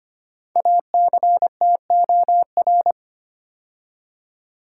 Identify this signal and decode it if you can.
Morse code, heard as a keyed tone, about 25 wpm, 700 Hz. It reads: ACTOR
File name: 5 Letter Words - First 3 Letters Given - 25wpm